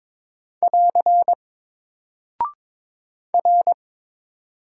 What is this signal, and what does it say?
Morse code, heard as a keyed tone, about 22 wpm, 700 Hz. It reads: <AR> R
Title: Single Character (Letter:Number:Punctuation:Prosign) Alt 22wpm